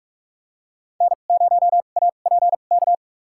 Morse code, heard as a keyed tone, about 46 wpm, spaced as for 33 wpm, 700 Hz. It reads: N0APK
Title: USA Call Signs - 5 Characters - 2x Character Spacing - 45wpm